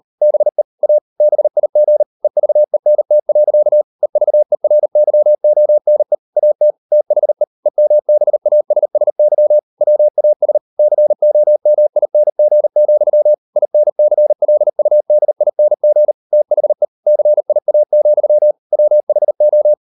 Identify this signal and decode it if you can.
Morse code, heard as a keyed tone, about 38 wpm, 600 Hz. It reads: BE A BIG EVENT. EVERYONE AT THE EMBASSY WAS COMING, INCLUDING THE CIA, WHO